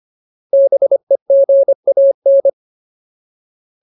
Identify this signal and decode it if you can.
Morse code, heard as a keyed tone, about 25 wpm, 550 Hz. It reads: BEGAN
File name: Ultimate Word List / 25WPM Ultimate